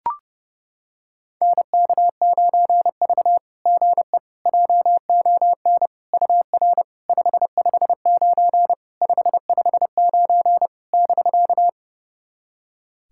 Morse code, heard as a keyed tone, about 30 wpm, 690 Hz. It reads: NK9V GE JON UR 559 559 <BK>
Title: SOTA QSO Practice 30wpm